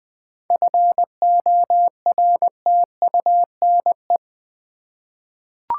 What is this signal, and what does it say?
Morse code, heard as a keyed tone, about 20 wpm, 700 Hz. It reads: FORTUNE
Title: Seven Letter Words - 20wpm